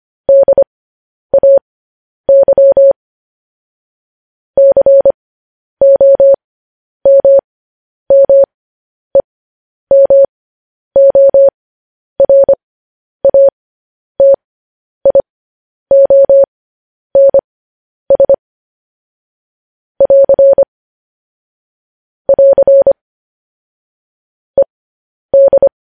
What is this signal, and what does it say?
Morse code, heard as a keyed tone, about 25 wpm, spaced as for 10 wpm, 550 Hz. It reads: DAY COMMEMORATIONS <AR> <AR> ED
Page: news-headlines-in-morse-code-at-10-wpm